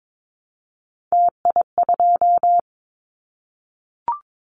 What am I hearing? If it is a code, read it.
Morse code, heard as a keyed tone, about 22 wpm, 700 Hz. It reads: TI2
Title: Callsign Prefixes for Top 100 DXCC - 22wpm